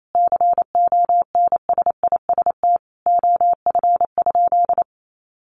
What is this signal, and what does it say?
Morse code, heard as a keyed tone, about 28 wpm, 700 Hz. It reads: CONSIST OF?
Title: Sentences from Top 1400 Words 28wpm